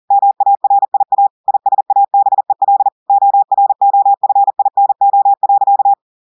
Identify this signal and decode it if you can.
Morse code, heard as a keyed tone, about 40 wpm, 800 Hz. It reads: MARIA ISABEL OROFINO.
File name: audios / portugais / portugais texte en clair 1 vitesse 40 mots minutes